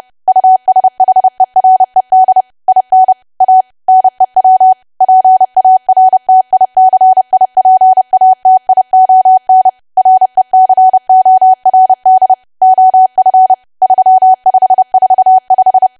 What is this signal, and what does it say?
Morse code, heard as a keyed tone, about 30 wpm, 750 Hz. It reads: USHERED IN A NEW PARTICIPATION RECORD OF 3545
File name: cwa / W1AW-Level3 / 30WPM